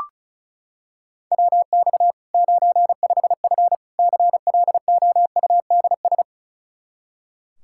Morse code, heard as a keyed tone, about 36 wpm, 700 Hz. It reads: WX 95F CLOUDS
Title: K1USN SST QSO Practice 35wpm